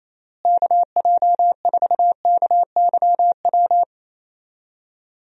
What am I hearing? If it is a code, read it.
Morse code, heard as a keyed tone, about 28 wpm, 700 Hz. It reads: KJ4KYW
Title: USA Call Signs - 6 Characters - Repeated 3x - 28wpm